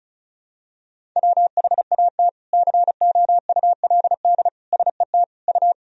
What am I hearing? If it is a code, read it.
Morse code, heard as a keyed tone, about 34 wpm, 700 Hz. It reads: WHAT COULD SET U
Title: Sentences from Top 200 Words and Abbreviations 35wpm